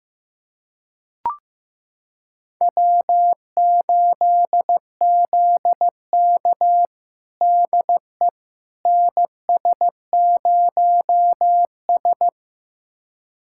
Morse code, heard as a keyed tone, about 15 wpm, 700 Hz. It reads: W8ZK DE NS0S
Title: USA Callsign DE Callsign - 4 Characters - 15wpm